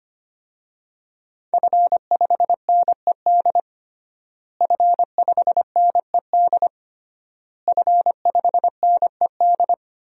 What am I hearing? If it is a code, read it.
Morse code, heard as a keyed tone, about 25 wpm, 700 Hz. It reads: F5NED F5NED F5NED
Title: DXCC-100 Call Signs - 5 Characters - Repeated 3x - 25wpm